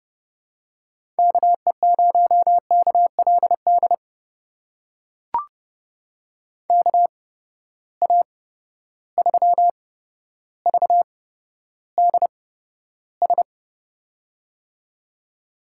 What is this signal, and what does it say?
Morse code, heard as a keyed tone, about 29 wpm, spaced as for 13 wpm, 700 Hz. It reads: KE0KLD KA3VDS
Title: USA Call Signs - 6 Characters - 8x Character Spacing - 30wpm